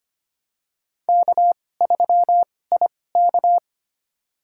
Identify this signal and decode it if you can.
Morse code, heard as a keyed tone, about 25 wpm, spaced as for 18 wpm, 700 Hz. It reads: K3IK
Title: USA Call Signs - 4 Characters - 2x Character Spacing - 25wpm